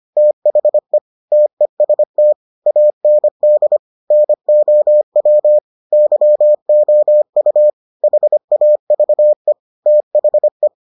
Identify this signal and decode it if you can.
Morse code, heard as a keyed tone, about 25 wpm, 600 Hz. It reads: THE TEST AND NOW YOU HAVE THE